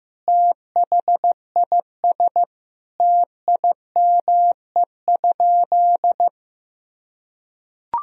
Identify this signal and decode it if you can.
Morse code, heard as a keyed tone, about 15 wpm, 700 Hz. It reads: THIS TIME?
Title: Sentences from Top 1000 Words - Review Part 2 - 15wpm